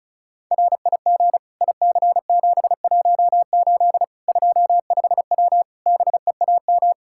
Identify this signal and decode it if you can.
Morse code, heard as a keyed tone, about 34 wpm, 700 Hz. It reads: RIG IC718 25W BEAM
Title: K1USN SST QSO Practice 35wpm